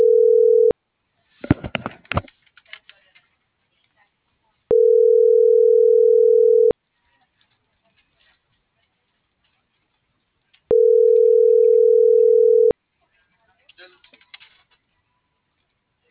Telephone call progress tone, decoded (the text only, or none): ringback tone